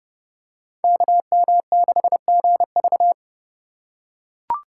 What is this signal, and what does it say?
Morse code, heard as a keyed tone, about 32 wpm, 700 Hz. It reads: KM6GV
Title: USA Call Signs - 5 Characters - 3x Character Spacing - 30wpm